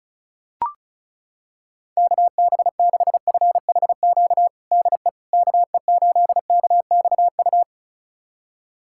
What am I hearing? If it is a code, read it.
Morse code, heard as a keyed tone, about 34 wpm, 700 Hz. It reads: KB6FHQ DE KE8KXU
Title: USA Callsign DE Callsign 35wpm